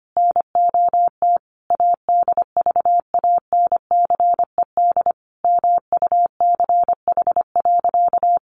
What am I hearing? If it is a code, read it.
Morse code, heard as a keyed tone, about 25 wpm, 700 Hz. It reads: NOT ADVANCED MUCH.
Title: Sentences from Top 1200 Words 25wpm